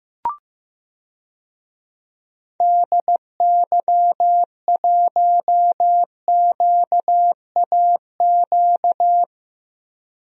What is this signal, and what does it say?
Morse code, heard as a keyed tone, about 15 wpm, 700 Hz. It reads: DY1QAQ
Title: DXCC-100 Call Signs - 6 Characters - Repeated 2x - 15wpm